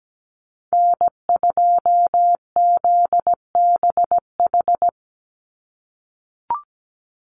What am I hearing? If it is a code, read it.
Morse code, heard as a keyed tone, about 17 wpm, 700 Hz. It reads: N2ZBH
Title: USA Call Signs 17wpm